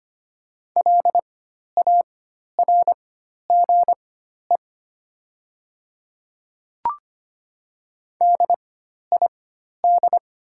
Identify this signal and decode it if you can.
Morse code, heard as a keyed tone, about 25 wpm, spaced as for 12 wpm, 700 Hz. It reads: LARGE DID
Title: 4x Character Spacing - Top 200 Words - 25wpm